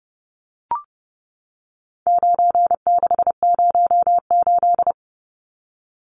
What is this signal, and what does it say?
Morse code, heard as a keyed tone, about 30 wpm, 700 Hz. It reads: 9608
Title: Numbers - 4-Digit 30wpm